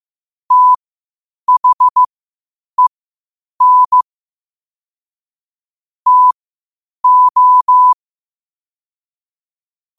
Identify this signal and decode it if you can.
Morse code, heard as a keyed tone, about 15 wpm, spaced as for 8 wpm, 1000 Hz. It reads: THEN TO
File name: qotdfiles / 8wpm